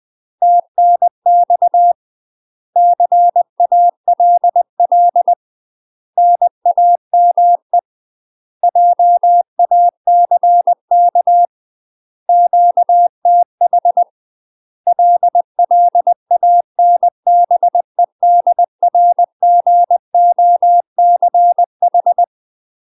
Morse code, heard as a keyed tone, about 20 wpm, 700 Hz. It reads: TNX CALL NAME JACK QTH LLANBEDRGOCH
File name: UK MASTER QSOs TRAINING / 20WPM / UK MASTER QSOs 0039